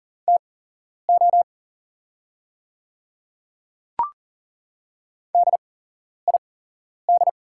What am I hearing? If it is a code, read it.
Morse code, heard as a keyed tone, about 40 wpm, spaced as for 11 wpm, 700 Hz. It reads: TO DID
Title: 8x Character Spacing - Top 100 Words - 40wpm